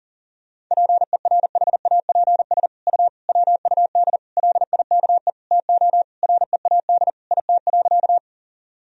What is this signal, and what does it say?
Morse code, heard as a keyed tone, about 40 wpm, 700 Hz. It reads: PERHAPS U WUD LIKE TO READ IT.
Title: Sentences from Top 500 Words and Abbreviations 40wpm